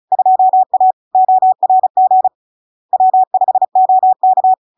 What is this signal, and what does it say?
Morse code, heard as a keyed tone, about 36 wpm, 750 Hz. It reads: 2A ORG W5OK
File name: LARK-FD-2022-Log-in-CW-35wpm-750-Hz